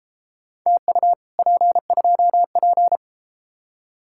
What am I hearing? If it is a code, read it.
Morse code, heard as a keyed tone, about 30 wpm, 700 Hz. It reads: TU P2P